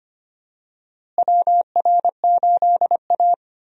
Morse code, heard as a keyed tone, about 25 wpm, 700 Hz. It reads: WR8A
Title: USA Call Signs - 4 Characters - Repeated 2x - 25wpm